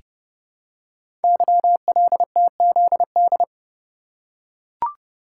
Morse code, heard as a keyed tone, about 30 wpm, 700 Hz. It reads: YLTZD
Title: Code Groups - 5 Letter:Numbers - 30wpm